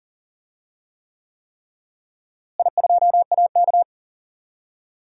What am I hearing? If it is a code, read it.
Morse code, heard as a keyed tone, about 40 wpm, 700 Hz. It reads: I2AK
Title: DXCC-100 Call Signs - 4 Characters - Repeated 3x - 40wpm